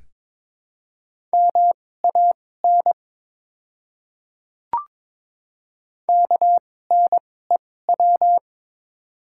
Morse code, heard as a keyed tone, about 22 wpm, spaced as for 16 wpm, 700 Hz. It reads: MAN KNEW
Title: 2x Character Spacing - Top 500 Words - 22wpm